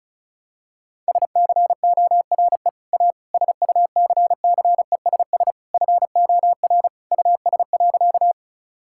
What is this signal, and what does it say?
Morse code, heard as a keyed tone, about 34 wpm, 700 Hz. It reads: SCORE A SUCCESS FOR US.
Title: Sentences from Top 1600 Words 35wpm